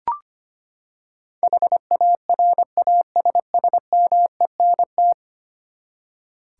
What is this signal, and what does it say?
Morse code, heard as a keyed tone, about 25 wpm, 700 Hz. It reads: HARASSMENT
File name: Ten Letter Words - 25wpm